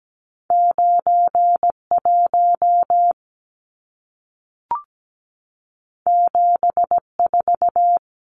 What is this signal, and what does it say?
Morse code, heard as a keyed tone, about 17 wpm, 700 Hz. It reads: 91 74
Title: Numbers - 2-Digit 17wpm